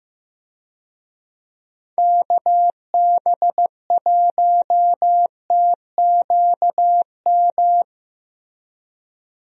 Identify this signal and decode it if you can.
Morse code, heard as a keyed tone, about 15 wpm, 700 Hz. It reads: KB1TQM
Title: USA Call Signs - 6 Characters - Repeated 3x - 15wpm